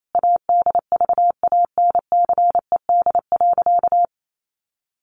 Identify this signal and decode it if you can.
Morse code, heard as a keyed tone, about 28 wpm, 700 Hz. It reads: ADVANCED.
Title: Sentences from Top 1200 Words 28wpm